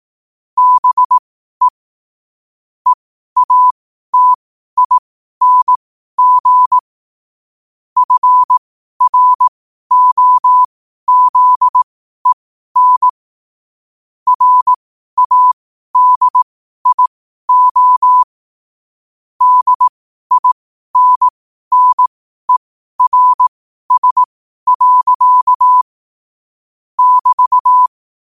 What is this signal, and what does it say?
Morse code, heard as a keyed tone, about 18 wpm, spaced as for 12 wpm, 1000 Hz. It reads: BE EATING FROZEN RADIO DINNERS. <BT>
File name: qotdfiles / 12wpm